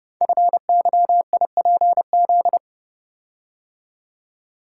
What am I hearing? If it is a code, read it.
Morse code, heard as a keyed tone, about 30 wpm, 700 Hz. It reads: FYIPZ
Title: Code Groups - 5 Letter:Numbers - 30wpm